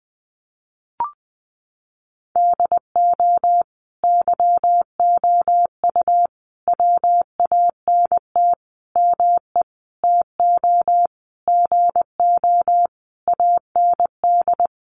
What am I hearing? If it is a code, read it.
Morse code, heard as a keyed tone, about 20 wpm, 700 Hz. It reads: DO YOU WANT ME TO GO AND
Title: Sentences from Top 200 Words 20wpm